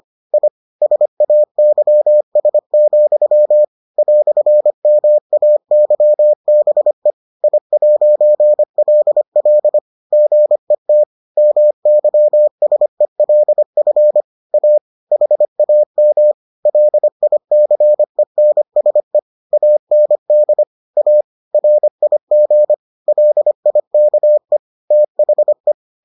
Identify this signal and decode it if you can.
Morse code, heard as a keyed tone, about 25 wpm, 600 Hz. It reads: I SAYS, "MAYBE I'LL GET MYSELF A HAM LICENSE AND A RIG LIKE THE